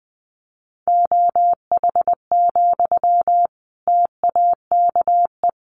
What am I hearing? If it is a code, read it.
Morse code, heard as a keyed tone, about 20 wpm, 700 Hz. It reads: OH, TAKE
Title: Sentences from Top 500 Words 20wpm